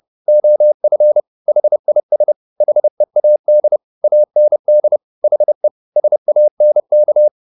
Morse code, heard as a keyed tone, about 30 wpm, 600 Hz. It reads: OF HIS HEAD AND HE SANK